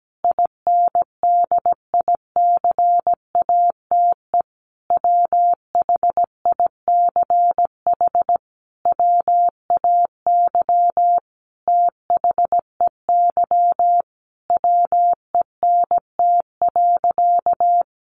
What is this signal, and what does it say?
Morse code, heard as a keyed tone, about 17 wpm, 700 Hz. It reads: INDICATE WHICH WAY THEY WENT.
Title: Sentences from Top 1300 Words 17wpm